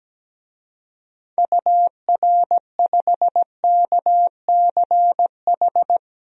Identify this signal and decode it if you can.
Morse code, heard as a keyed tone, about 17 wpm, 700 Hz. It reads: UR5KCH
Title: DXCC-100 Call Signs - 6 Characters - 17wpm